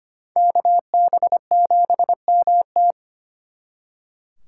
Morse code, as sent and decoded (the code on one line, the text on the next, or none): -.- -... --... -- -
KB7MT